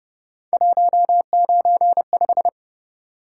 Morse code, heard as a keyed tone, about 30 wpm, 700 Hz. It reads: 195